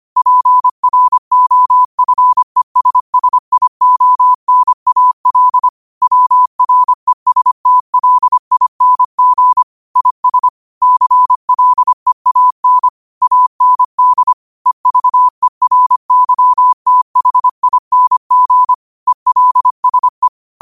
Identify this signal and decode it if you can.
Morse code, heard as a keyed tone, about 25 wpm, 1000 Hz. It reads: PROFESSIONAL WRESTLING IS CLEAN AND EVERYTHING ELSE
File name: qotdfiles / 25wpm